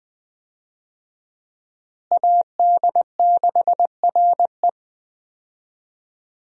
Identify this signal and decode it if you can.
Morse code, heard as a keyed tone, about 20 wpm, 700 Hz. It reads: AD6RE
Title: USA Call Signs - 5 Characters - Repeated 3x - 20wpm